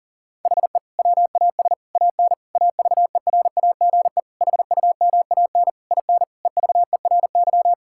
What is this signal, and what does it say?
Morse code, heard as a keyed tone, about 40 wpm, 700 Hz. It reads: HE WAS AN AVERAGE HUMAN IN EVERY
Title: Sentences from Top 1000 Words - Review Part 6 - 40wpm